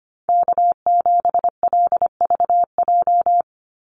Morse code, heard as a keyed tone, about 25 wpm, 700 Hz. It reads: K7LVJ
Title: Call Signs - 25wpm